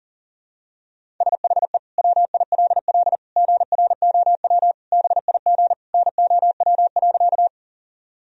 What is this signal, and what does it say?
Morse code, heard as a keyed tone, about 40 wpm, 700 Hz. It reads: SHE WILL GROW BIG NOW.
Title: Sentences from Top 1000 Words - Review Part 1 - 40wpm